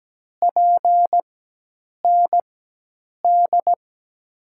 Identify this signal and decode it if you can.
Morse code, heard as a keyed tone, about 17 wpm, spaced as for 8 wpm, 700 Hz. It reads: PND
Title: USA Call Signs - 4x Character Spacing - 17wpm